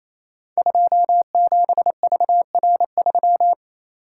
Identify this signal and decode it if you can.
Morse code, heard as a keyed tone, about 27 wpm, 690 Hz. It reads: 27VR3
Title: Code Groups - 5 Letter:Numbers - 28wpm